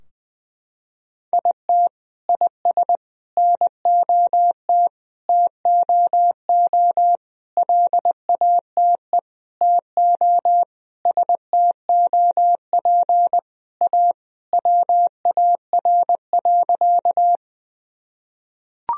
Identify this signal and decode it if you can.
Morse code, heard as a keyed tone, about 20 wpm, 700 Hz. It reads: IT IS NOT TOO LATE TO STOP A WAR.
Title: Sentences from Top 1000 Words - Review Part 2 - 20wpm